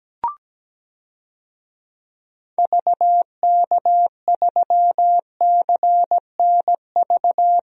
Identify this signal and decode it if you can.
Morse code, heard as a keyed tone, about 17 wpm, 700 Hz. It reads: VK3CNV
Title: DXCC-100 Call Signs - 6 Characters - 17wpm